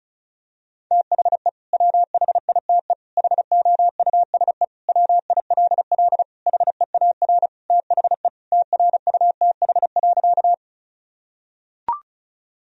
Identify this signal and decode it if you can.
Morse code, heard as a keyed tone, about 36 wpm, 700 Hz. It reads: THE WHITE HOUSE WILL HEAR THE TRUTH.
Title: Sentences from Top 1000 Words - Review Part 1 - 35wpm